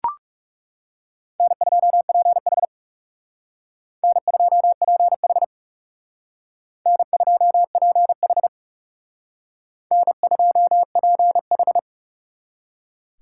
Morse code, heard as a keyed tone, about 36 wpm, 700 Hz. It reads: N2PH N2PH N2PH N2PH
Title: Speed Racing - USA 1x2 Call Signs 30wpm